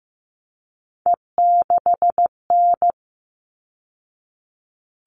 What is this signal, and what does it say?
Morse code, heard as a keyed tone, about 15 wpm, 700 Hz. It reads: E6N
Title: All Cut Numbers - 3-Digit 15wpm